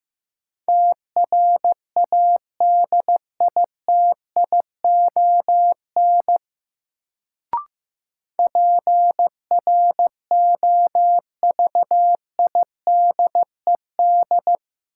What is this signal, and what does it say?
Morse code, heard as a keyed tone, about 16 wpm, 700 Hz. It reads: TRADITION PROVIDED
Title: Sentences from Top 1500 Words - New Word Drill - 15wpm